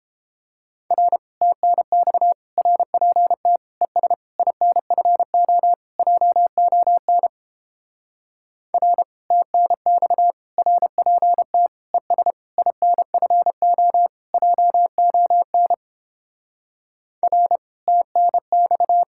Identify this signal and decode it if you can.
Morse code, heard as a keyed tone, about 31 wpm, 700 Hz. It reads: R TNX RPT ES INFO JON R TNX RPT ES INFO JON R TNX